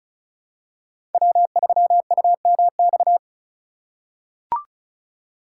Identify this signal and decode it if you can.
Morse code, heard as a keyed tone, about 36 wpm, 700 Hz. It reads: W3UMX
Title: USA Call Signs - 5 Characters - 2x Character Spacing - 35wpm